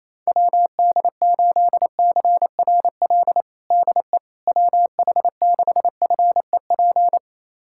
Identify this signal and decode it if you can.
Morse code, heard as a keyed tone, about 27 wpm, 700 Hz. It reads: WD8CRL DE WH6FEP
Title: USA Callsign DE Callsign - 6 Characters - 28wpm